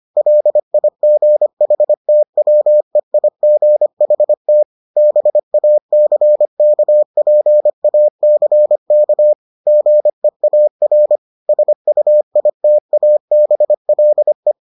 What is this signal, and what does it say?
Morse code, heard as a keyed tone, about 25 wpm, 600 Hz. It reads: LIGHTWEIGHT BACKPACK GEAR SUITABLE